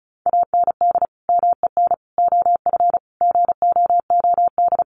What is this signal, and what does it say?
Morse code, heard as a keyed tone, about 34 wpm, 700 Hz. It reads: AND MEN OF GOOD
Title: Sentences from Top 1400 Words 35wpm